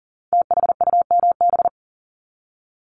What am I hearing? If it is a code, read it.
Morse code, heard as a keyed tone, about 38 wpm, 700 Hz. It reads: THUMB
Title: Multi-letter Phonemes 40wpm